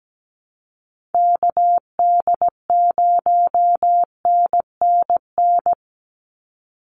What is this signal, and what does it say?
Morse code, heard as a keyed tone, about 17 wpm, 700 Hz. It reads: KD0NNN
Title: USA Call Signs 17wpm